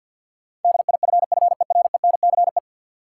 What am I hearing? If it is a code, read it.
Morse code, heard as a keyed tone, about 53 wpm, 700 Hz. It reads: DIFFERENCE
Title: Sentences from Top 800 Words - New Word Drill - 50wpm